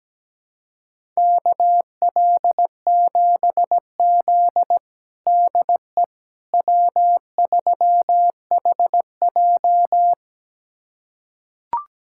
Morse code, heard as a keyed tone, about 17 wpm, 700 Hz. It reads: KL7Z DE W3HJ
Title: USA Callsign DE Callsign - 4 Characters - 17wpm